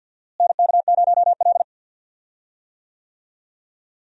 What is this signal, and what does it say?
Morse code, heard as a keyed tone, about 47 wpm, 700 Hz. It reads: NK0L
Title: USA Call Signs - 4 Characters - Repeated 3x - 50wpm